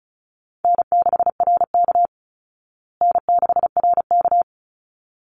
Morse code, heard as a keyed tone, about 34 wpm, 700 Hz. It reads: N6RK N6RK
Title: Easy Call Signs Repeated 2x - 35wpm